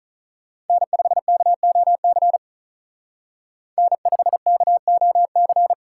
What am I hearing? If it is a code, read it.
Morse code, heard as a keyed tone, about 42 wpm, 700 Hz. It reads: N5KOC N5KOC
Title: Speed Racing - USA 1x3 Call Signs 35wpm